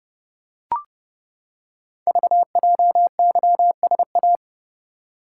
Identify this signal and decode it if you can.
Morse code, heard as a keyed tone, about 30 wpm, 700 Hz. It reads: VJYSA
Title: Code Groups - 5 Letters - 30wpm